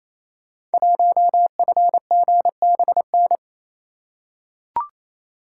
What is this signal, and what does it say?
Morse code, heard as a keyed tone, about 28 wpm, 700 Hz. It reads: 1FGBN
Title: Code Groups - 5 Letter:Numbers - 28wpm